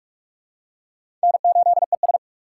Morse code, heard as a keyed tone, about 43 wpm, 700 Hz. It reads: N8ES
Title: USA Call Signs - 4 Characters - 6x Character Spacing - 45wpm